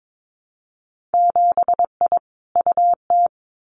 Morse code, heard as a keyed tone, about 22 wpm, 700 Hz. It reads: 7I UT
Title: Winter Field Day Exchanges Encore 22wpm